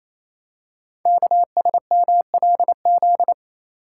Morse code, heard as a keyed tone, about 28 wpm, 700 Hz. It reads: KSMLZ
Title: Code Groups - 5 Letters - 28wpm